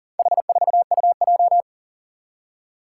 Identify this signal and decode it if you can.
Morse code, heard as a keyed tone, about 40 wpm, 700 Hz. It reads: H4UJ
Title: Four Challenging Letter-Number Combinations 40wpm